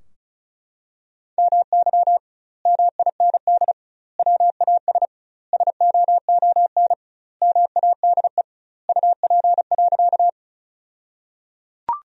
Morse code, heard as a keyed tone, about 34 wpm, 700 Hz. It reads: MY MIND WAS SOON MADE UP.
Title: 2x Word-Spacing - Sentences from Top 500 Words - 35wpm